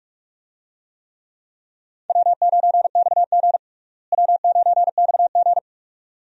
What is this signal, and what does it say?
Morse code, heard as a keyed tone, about 42 wpm, 700 Hz. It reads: W9XG W9XG
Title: USA Call Signs - 4 Characters - Repeated 2x - 45wpm